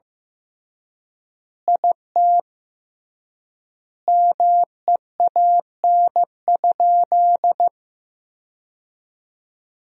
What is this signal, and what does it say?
Morse code, heard as a keyed tone, about 15 wpm, 700 Hz. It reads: IT MEAN?
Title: 3x Word-Spacing - Sentences from Top 200 Words - 15wpm